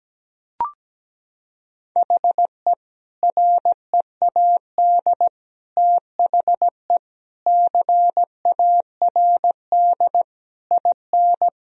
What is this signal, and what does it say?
Morse code, heard as a keyed tone, about 17 wpm, 700 Hz. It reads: HE READ THE CARD IN
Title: Sentences from Top 1000 Words - Review Part 4 - 17wpm